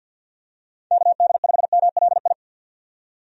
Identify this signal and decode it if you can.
Morse code, heard as a keyed tone, about 50 wpm, 700 Hz. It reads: KD5MLI
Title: USA Call Signs - 6 Characters - 50wpm